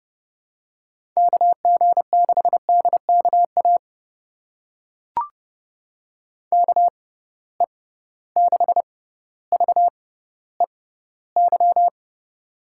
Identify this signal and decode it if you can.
Morse code, heard as a keyed tone, about 30 wpm, spaced as for 15 wpm, 700 Hz. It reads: KG6DKA KE6VEY
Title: USA Call Signs - 6 Characters - 6x Character Spacing - 30wpm